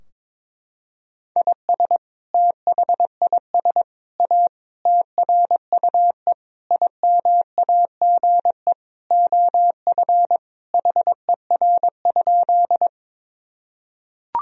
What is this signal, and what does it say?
Morse code, heard as a keyed tone, about 22 wpm, 700 Hz. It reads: IS THIS A TRUE IMAGE OF HER?
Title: Sentences from Top 1000 Words - Review Part 4 - 22wpm